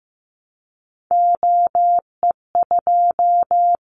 Morse code, heard as a keyed tone, about 15 wpm, 700 Hz. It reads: OE2
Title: Callsign Prefixes for Top 50 DXCC - 15wpm